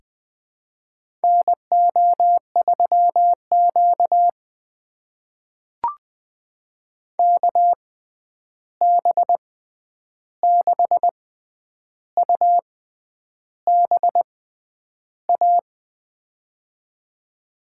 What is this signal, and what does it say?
Morse code, heard as a keyed tone, about 20 wpm, spaced as for 9 wpm, 700 Hz. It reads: NO3Q KB6UBA
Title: USA Call Signs - 6x Character Spacing - 20wpm